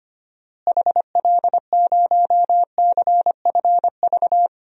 Morse code, heard as a keyed tone, about 25 wpm, 700 Hz. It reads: HL0CFV
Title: DXCC-100 Call Signs - 6 Characters - Repeated 2x - 25wpm